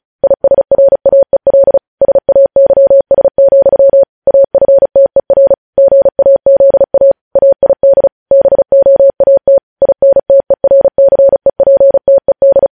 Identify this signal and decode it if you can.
Morse code, heard as a keyed tone, about 36 wpm, 550 Hz. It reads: ISRAEL SAYS, AFTER GAZA AID BOAT INTERCEPTED
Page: news headlines in morse code at 35 wpm